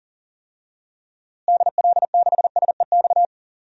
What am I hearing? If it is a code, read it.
Morse code, heard as a keyed tone, about 40 wpm, 700 Hz. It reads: DL6SEX
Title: DXCC-100 Call Signs - 6 Characters - Repeated 2x - 40wpm